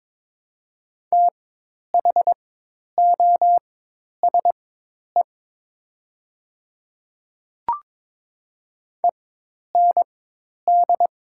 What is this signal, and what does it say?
Morse code, heard as a keyed tone, about 22 wpm, spaced as for 10 wpm, 700 Hz. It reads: THOSE END
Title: 4x Character Spacing - Top 300 Words - 22wpm